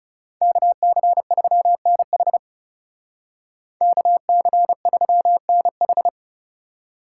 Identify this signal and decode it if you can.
Morse code, heard as a keyed tone, about 35 wpm, 700 Hz. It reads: KC3NH KC3NH
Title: Speed Racing - USA 2x2 Call Signs 30wpm